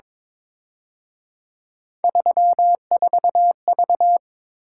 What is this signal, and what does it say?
Morse code, heard as a keyed tone, about 26 wpm, 700 Hz. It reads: 34V
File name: Speed-Racing - Three Letter-Number Combinations - 22wpm